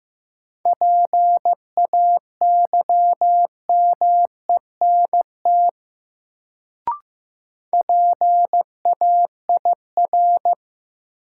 Morse code, heard as a keyed tone, about 15 wpm, 700 Hz. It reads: PAYMENT PAIR
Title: Sentences from Top 1200 Words - New Word Drill - 15wpm